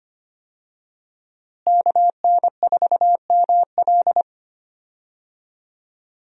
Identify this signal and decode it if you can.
Morse code, heard as a keyed tone, about 25 wpm, 700 Hz. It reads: KN4ML
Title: USA Call Signs - 5 Characters - Repeated 2x - 25wpm